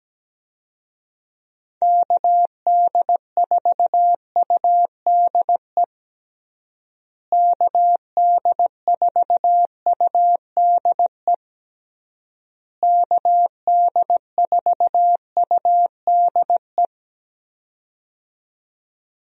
Morse code, heard as a keyed tone, about 17 wpm, 700 Hz. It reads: KD4UDE KD4UDE KD4UDE
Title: USA Call Signs - 6 Characters - Repeated 3x - 17wpm